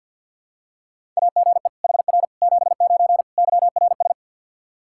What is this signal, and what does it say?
Morse code, heard as a keyed tone, about 51 wpm, 700 Hz. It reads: AGE HR 79 YRS
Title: K1USN SST QSO Practice 50wpm